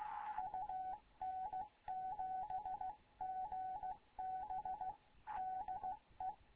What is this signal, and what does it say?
Morse code, heard as a keyed tone, about 15 wpm, 750 Hz. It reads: UN7GB DE
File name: Morze-HA2MN / hangp / 2 11 GYAKORLAT